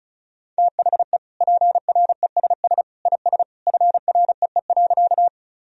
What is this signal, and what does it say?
Morse code, heard as a keyed tone, about 34 wpm, 700 Hz. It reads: THE PRESS IS FREE.
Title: Sentences from Top 1000 Words - Review Part 1 - 35wpm